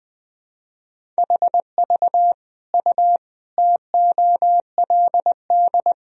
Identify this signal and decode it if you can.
Morse code, heard as a keyed tone, about 20 wpm, 700 Hz. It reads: HV U TOLD
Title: Sentences from Top 500 Words and Abbreviations 20wpm